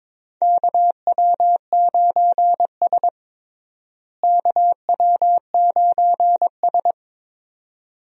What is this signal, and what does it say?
Morse code, heard as a keyed tone, about 22 wpm, 700 Hz. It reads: KW9S KW9S
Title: USA Call Signs - 4 Characters - Repeated 2x - 22wpm